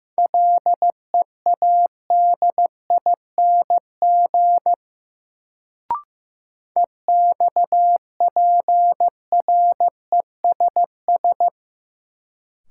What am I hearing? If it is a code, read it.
Morse code, heard as a keyed tone, about 15 wpm, 700 Hz. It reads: LEADING EXPRESS
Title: Sentences from Top 1100 Words - New Word Drill - 15wpm